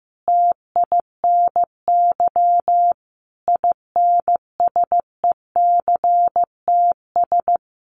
Morse code, heard as a keyed tone, about 15 wpm, 700 Hz. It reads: TINY INSECTS
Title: A Sound Of Thunder 15wpm